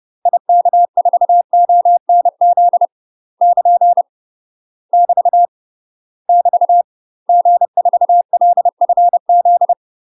Morse code, heard as a keyed tone, about 30 wpm, 700 Hz. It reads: IK4ONZ <KN> <BT> <BT> G4LFZ
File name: UK MASTER QSOs TRAINING / 30WPM / UK MASTER QSOs 0004